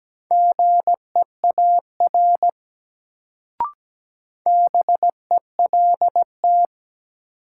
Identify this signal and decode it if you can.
Morse code, heard as a keyed tone, about 17 wpm, 700 Hz. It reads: GEAR BELT
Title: Sentences from Top 1400 Words - New Word Drill - 17wpm